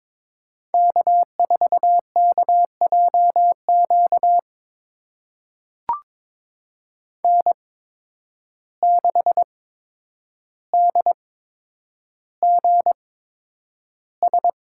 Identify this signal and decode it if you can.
Morse code, heard as a keyed tone, about 22 wpm, spaced as for 9 wpm, 700 Hz. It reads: K4KJQ N6DGS
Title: USA Call Signs - 5 Characters - 8x Character Spacing - 22wpm